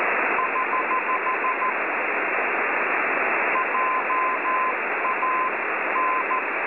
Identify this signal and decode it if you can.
Morse code, heard as a keyed tone, about 13 wpm, spaced as for 11 wpm, 1030 Hz, with radio noise. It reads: <HH> JAN
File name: cwronde / websdr recording .6kHz